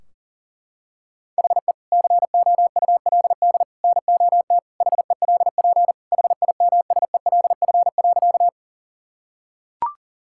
Morse code, heard as a keyed tone, about 40 wpm, 700 Hz. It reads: HE COULD NOT HELP HIMSELF.
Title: Sentences from Top 1600 Words 40wpm